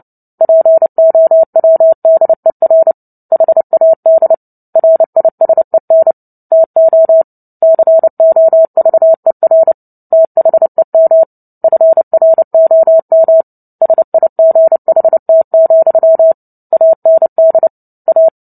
Morse code, heard as a keyed tone, about 30 wpm, 650 Hz. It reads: POWDER HAD RISEN TO COVER THEM FROM SIGHT, AND A
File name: Book Two / 03-Chapter 3 - 29 WPM